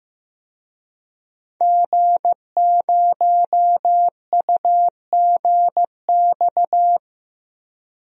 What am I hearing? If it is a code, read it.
Morse code, heard as a keyed tone, about 15 wpm, 700 Hz. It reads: G0UGX
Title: DXCC-100 Call Signs - 5 Characters - Repeated 2x - 15wpm